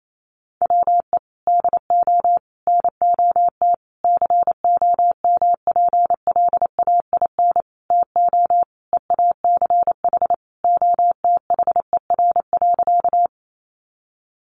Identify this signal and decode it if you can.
Morse code, heard as a keyed tone, about 28 wpm, 700 Hz. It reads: WE DO NOT COMPLAIN TO EACH OTHER.
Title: Sentences from Top 1300 Words 28wpm